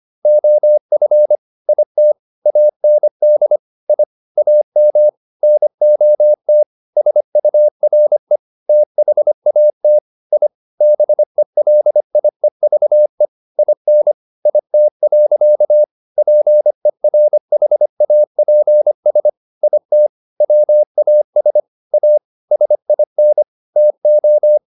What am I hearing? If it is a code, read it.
Morse code, heard as a keyed tone, about 25 wpm, 600 Hz. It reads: OF IT AND I AM NOT SURE THAT I BELIEVE IN IT. PERHAPS IT WAS A SIN TO